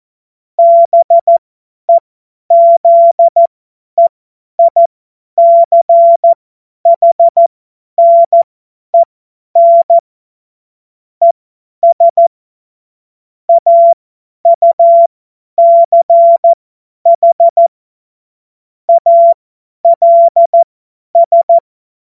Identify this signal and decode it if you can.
Morse code, heard as a keyed tone, about 14 wpm, spaced as for 10 wpm, 680 Hz. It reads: BEZEICHNEN ES AUCH ALS
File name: Morsekurs / Witze / Witz 12 Dezember 2023 50 F70